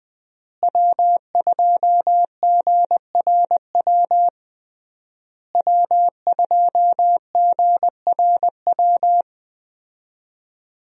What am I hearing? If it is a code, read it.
Morse code, heard as a keyed tone, about 21 wpm, 700 Hz. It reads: W2GRW W2GRW
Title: USA Call Signs - 5 Characters - Repeated 3x - 20wpm